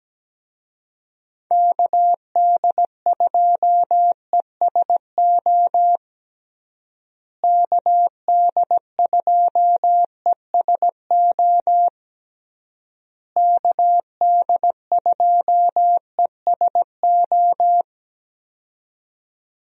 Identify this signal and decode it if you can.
Morse code, heard as a keyed tone, about 17 wpm, 700 Hz. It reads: KD2ESO KD2ESO KD2ESO
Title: USA Call Signs - 6 Characters - Repeated 3x - 17wpm